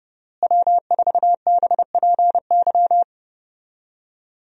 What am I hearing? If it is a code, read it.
Morse code, heard as a keyed tone, about 30 wpm, 700 Hz. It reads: W4BPY
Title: USA Call Signs - 5 Characters - Repeated 3x - 30wpm